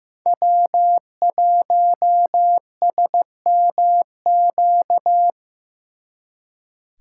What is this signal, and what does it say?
Morse code, heard as a keyed tone, about 19 wpm, 700 Hz. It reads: W1SMQ
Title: Speed Racing - USA 1x3 Call Signs 15wpm